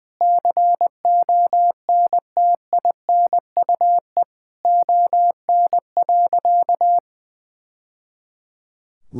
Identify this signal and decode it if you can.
Morse code, heard as a keyed tone, about 21 wpm, 700 Hz. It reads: CONTINUE ON.
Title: Sentences from Top 1000 Words - Review Part 6 - 20wpm